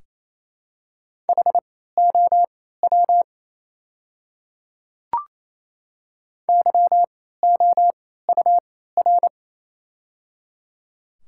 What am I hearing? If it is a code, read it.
Morse code, heard as a keyed tone, about 28 wpm, spaced as for 16 wpm, 700 Hz. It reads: HOW YOUR
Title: 3x Character Spacing - Top 100 Words - 28wpm